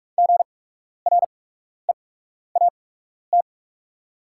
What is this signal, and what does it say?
Morse code, heard as a keyed tone, about 44 wpm, spaced as for 12 wpm, 700 Hz. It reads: GREAT
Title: 8x Character Spacing - Top 300 Words - 45wpm